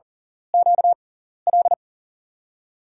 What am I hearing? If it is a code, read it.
Morse code, heard as a keyed tone, about 40 wpm, spaced as for 14 wpm, 700 Hz. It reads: QL
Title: USA Call Signs - 6 Characters - 6x Character Spacing - 40wpm